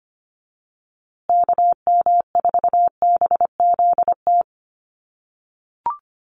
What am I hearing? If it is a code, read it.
Morse code, heard as a keyed tone, about 25 wpm, 700 Hz. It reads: KM4BZT
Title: USA Call Signs Repeated 2x 25wpm